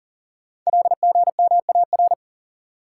Morse code, heard as a keyed tone, about 39 wpm, 700 Hz. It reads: LGMAR
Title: Code Groups - 5 Letters - 40wpm